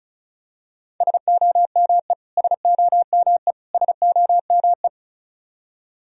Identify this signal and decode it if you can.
Morse code, heard as a keyed tone, about 36 wpm, 700 Hz. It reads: SOME SOME SOME
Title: Top 100 Words Repeated 35wpm